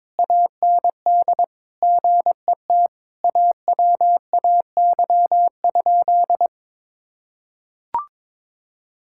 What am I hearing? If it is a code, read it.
Morse code, heard as a keyed tone, about 22 wpm, 700 Hz. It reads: AND GET AWAY?
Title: Sentences from Top 1000 Words - Review Part 6 - 22wpm